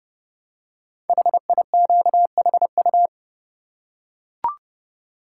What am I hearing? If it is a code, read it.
Morse code, heard as a keyed tone, about 30 wpm, 700 Hz. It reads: HIQHU
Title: Code Groups - 5 Letters - 30wpm